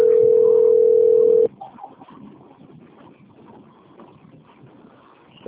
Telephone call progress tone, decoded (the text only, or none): ringback tone